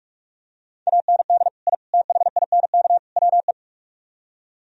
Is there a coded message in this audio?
Yes: Morse code from 0.9 to 3.5 s, heard as a keyed tone, about 47 wpm, 700 Hz. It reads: AND I THINK WE